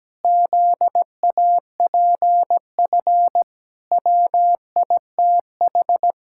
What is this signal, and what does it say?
Morse code, heard as a keyed tone, about 17 wpm, 700 Hz. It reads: ZAPF WITH
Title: Pangram Challenge v2 - 17wpm